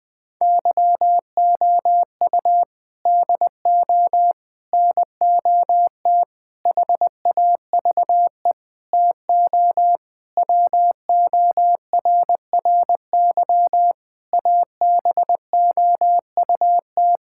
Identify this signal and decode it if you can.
Morse code, heard as a keyed tone, about 20 wpm, 700 Hz. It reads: YOU DO NOT HAVE TO WORRY ABOUT
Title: Sentences with 5-Word Phrases - Spoken Beginning - 20wpm